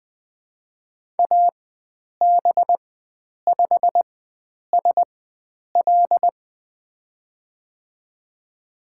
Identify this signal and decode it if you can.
Morse code, heard as a keyed tone, about 20 wpm, spaced as for 9 wpm, 700 Hz. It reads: AB5SL
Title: USA Call Signs - 5 Characters - 4x Character Spacing - 20wpm